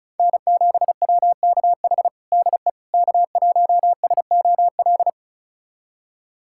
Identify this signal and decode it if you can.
Morse code, heard as a keyed tone, about 36 wpm, 700 Hz. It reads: N7WKH DE K1SOL
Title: USA Callsign DE Callsign - 5 Characters - 35wpm